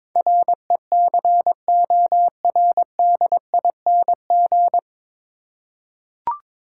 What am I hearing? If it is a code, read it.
Morse code, heard as a keyed tone, about 22 wpm, 700 Hz. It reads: RECORDING
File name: 9 Letter Words - First 3 Letters Given - 22wpm